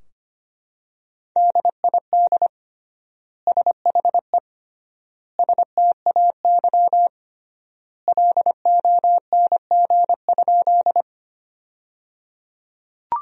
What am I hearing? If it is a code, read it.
Morse code, heard as a keyed tone, about 25 wpm, 700 Hz. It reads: DID SHE STAY LONG?
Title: 3x Word-Spacing - Sentences from Top 500 Words - 25wpm